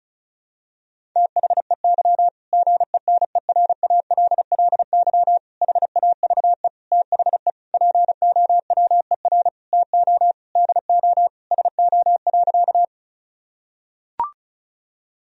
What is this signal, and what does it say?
Morse code, heard as a keyed tone, about 36 wpm, 700 Hz. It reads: THEY GENERALLY HAVE THE POWER TO DO SO.
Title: Sentences from Top 1000 Words - Review Part 4 - 35wpm